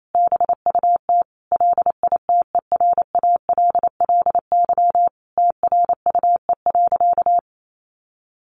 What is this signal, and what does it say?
Morse code, heard as a keyed tone, about 28 wpm, 700 Hz. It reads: BUT LITERALLY TRUE.
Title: Sentences from Top 1400 Words 28wpm